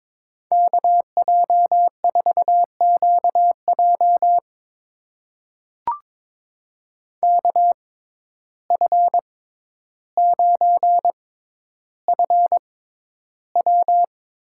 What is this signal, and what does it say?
Morse code, heard as a keyed tone, about 21 wpm, spaced as for 11 wpm, 700 Hz. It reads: KJ4QJ KF9FW
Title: USA Call Signs - 5 Characters - 6x Character Spacing - 22wpm